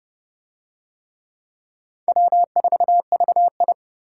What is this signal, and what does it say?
Morse code, heard as a keyed tone, about 29 wpm, 700 Hz. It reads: W4VI